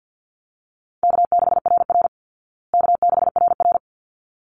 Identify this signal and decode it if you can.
Morse code, heard as a keyed tone, about 50 wpm, 700 Hz. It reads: K6RR K6RR
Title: Easy Call Signs Repeated 2x - 50wpm